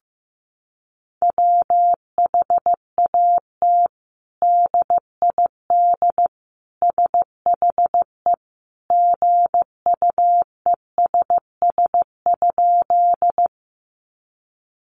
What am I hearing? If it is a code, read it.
Morse code, heard as a keyed tone, about 15 wpm, 700 Hz. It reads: WHAT DID SHE GUESS?
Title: Sentences from Top 1500 Words 15wpm